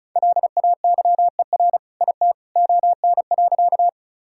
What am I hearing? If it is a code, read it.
Morse code, heard as a keyed tone, about 36 wpm, 700 Hz. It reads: LAYER IT ON.
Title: Sentences from Top 1600 Words 35wpm